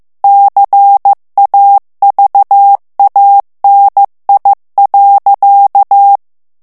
Morse code, heard as a keyed tone, about 15 wpm, 800 Hz. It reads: CAVANI.
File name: ITALIEN texte clair 1 vitesse 14 mots minute